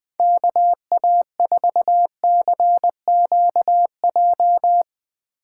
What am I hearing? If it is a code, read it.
Morse code, heard as a keyed tone, about 20 wpm, 700 Hz. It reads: KA4CQJ
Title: Speed Racing - USA 2x3 Call Signs 17wpm